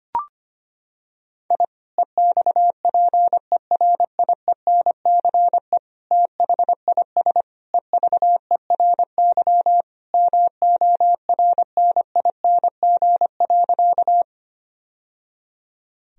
Morse code, heard as a keyed tone, about 25 wpm, 700 Hz. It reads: I EXPERIENCE THIS EVERY MORNING.
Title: Sentences from Top 1000 Words - Review Part 3 - 25wpm